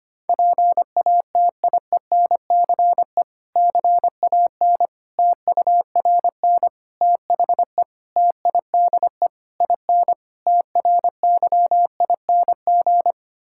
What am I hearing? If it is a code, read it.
Morse code, heard as a keyed tone, about 25 wpm, 700 Hz. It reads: PATIENCE CAN TURN THE TIDE IN TRYING
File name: practice / 25wpm / Difficult Sentences with Hints - 25wpm